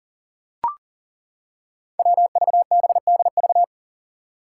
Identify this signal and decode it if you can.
Morse code, heard as a keyed tone, about 40 wpm, 700 Hz. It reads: WVBDV
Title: Five Challenging Letter-Number Combinations 40wpm